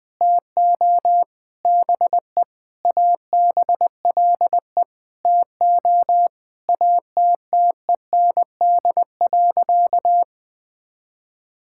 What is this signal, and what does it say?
Morse code, heard as a keyed tone, about 21 wpm, 700 Hz. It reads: TO BE ABLE TO ATTEND.
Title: Sentences with 7-Word Phrases - 20wpm